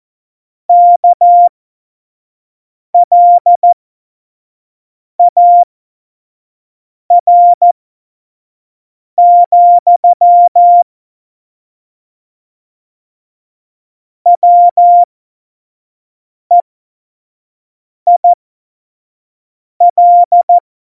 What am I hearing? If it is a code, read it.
Morse code, heard as a keyed tone, about 14 wpm, spaced as for 5 wpm, 700 Hz. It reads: KLAR, WEIL
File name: Morsekurs / Witze / Witz 08. August 17, Tempo 25 F70